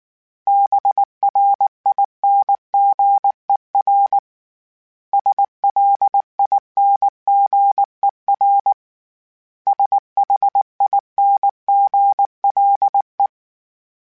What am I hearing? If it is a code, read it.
Morse code, heard as a keyed tone, about 19 wpm, 800 Hz. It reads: BRINGER SLINGER SHINGLE
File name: Special INGs / Special INGs-mp3s / Hidden ings @18